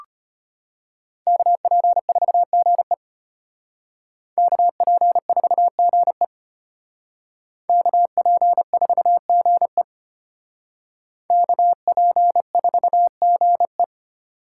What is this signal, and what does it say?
Morse code, heard as a keyed tone, about 32 wpm, 700 Hz. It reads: KP4GE KP4GE KP4GE KP4GE